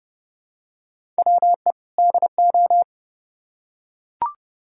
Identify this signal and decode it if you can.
Morse code, heard as a keyed tone, about 28 wpm, 700 Hz. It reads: WE DO